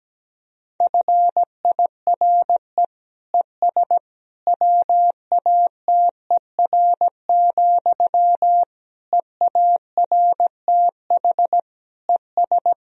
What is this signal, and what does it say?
Morse code, heard as a keyed tone, about 17 wpm, 700 Hz. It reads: FIRE ES WATER, EARTH ES
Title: Sentences from Top 400 Words and Abbreviations 17wpm